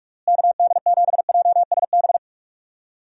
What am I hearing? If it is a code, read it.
Morse code, heard as a keyed tone, about 44 wpm, 700 Hz. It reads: KD7JSB
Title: USA Call Signs - 6 Characters - Repeated 3x - 45wpm